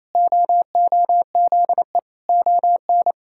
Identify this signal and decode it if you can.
Morse code, heard as a keyed tone, about 28 wpm, 700 Hz. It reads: OOZE ON
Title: Pangram Challenge v2 - 28wpm